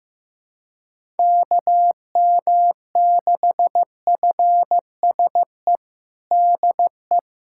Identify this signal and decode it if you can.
Morse code, heard as a keyed tone, about 15 wpm, 700 Hz. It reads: KM6FSE DE
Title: USA Callsign DE Callsign - 6 Characters - 15wpm